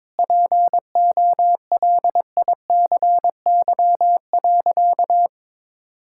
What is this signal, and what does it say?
Morse code, heard as a keyed tone, about 22 wpm, 700 Hz. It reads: POLICY.
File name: practice / 22wpm / Sentences with 8-Word Phrases - 22wpm